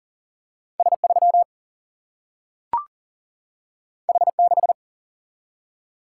Morse code, heard as a keyed tone, about 40 wpm, 700 Hz. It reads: S3 H6
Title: Single-Letter Single-Number 40wpm